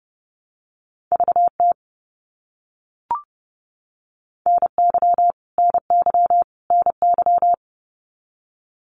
Morse code, heard as a keyed tone, about 30 wpm, 700 Hz. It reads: VT NY NY NY
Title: US States Repeated 3x 30wpm